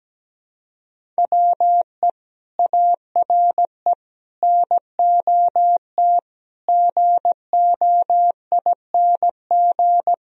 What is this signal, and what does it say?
Morse code, heard as a keyed tone, about 17 wpm, 700 Hz. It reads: WE ARE NOT GOING
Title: Speed-Racing - Sets of 5 Words 17wpm